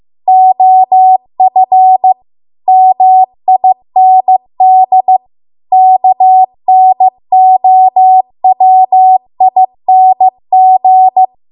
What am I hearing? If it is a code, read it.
Morse code, heard as a keyed tone, about 15 wpm, 750 Hz. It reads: OF MIND KNOWING